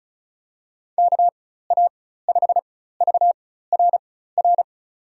Morse code, heard as a keyed tone, about 34 wpm, spaced as for 16 wpm, 700 Hz. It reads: KA5VRR
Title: USA Call Signs - 6 Characters - 4x Character Spacing - 35wpm